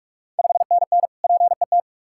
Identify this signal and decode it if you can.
Morse code, heard as a keyed tone, about 46 wpm, 690 Hz. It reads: 5NN PET
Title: Ontario QSO Party Exchange 45wpm